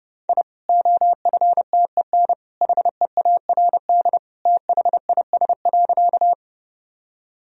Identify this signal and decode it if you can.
Morse code, heard as a keyed tone, about 31 wpm, 700 Hz. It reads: I OFTEN HEARD THIS.
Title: Sentences from Top 1000 Words - Review Part 1 - 30wpm